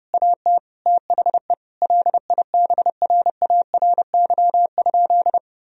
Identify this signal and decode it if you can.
Morse code, heard as a keyed tone, about 32 wpm, 700 Hz. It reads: AT THE LIBRARY?
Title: Sentences from Top 1000 Words - Review Part 4 - 30wpm